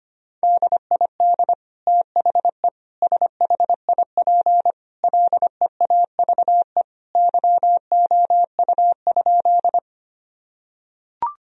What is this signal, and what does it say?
Morse code, heard as a keyed tone, about 25 wpm, 700 Hz. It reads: DID THE SHIP LEAVE YOU?
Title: Sentences from Top 1000 Words - Review Part 2 - 25wpm